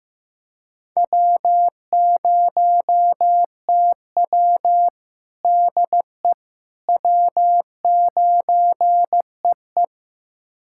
Morse code, heard as a keyed tone, about 15 wpm, 700 Hz. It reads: W0TW DE W9EE
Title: USA Callsign DE Callsign - 4 Characters - 15wpm